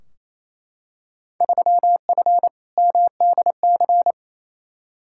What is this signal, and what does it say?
Morse code, heard as a keyed tone, about 28 wpm, 700 Hz. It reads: 3F MDC
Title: Field Day Exchanges Encore - 28wpm